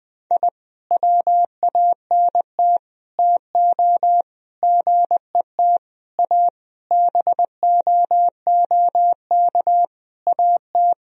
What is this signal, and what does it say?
Morse code, heard as a keyed tone, about 21 wpm, 700 Hz. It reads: I WANT TO GET A BOOK AT
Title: Sentences from Top 1000 Words - Review Part 1 - 20wpm